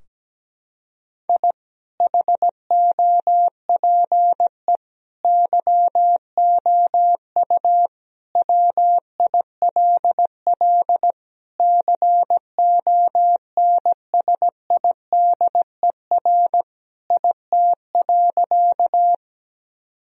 Morse code, heard as a keyed tone, about 17 wpm, 700 Hz. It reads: I HOPE YOU WILL CONSIDER IT.
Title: Sentences from Top 1000 Words - Review Part 4 - 17wpm